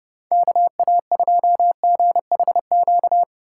Morse code, heard as a keyed tone, about 30 wpm, 700 Hz. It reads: KA2GHQ